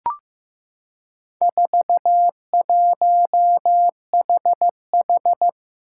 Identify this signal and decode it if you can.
Morse code, heard as a keyed tone, about 15 wpm, 700 Hz. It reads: 41HH
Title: Four Challenging Letter-Number Combinations 15wpm